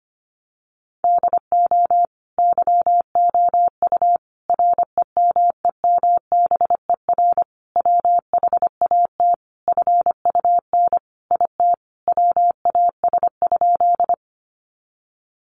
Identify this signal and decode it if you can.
Morse code, heard as a keyed tone, about 25 wpm, 700 Hz. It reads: DO YOU REMEMBER WHAT FUN IT WAS?
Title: Sentences from Top 700 Words 25wpm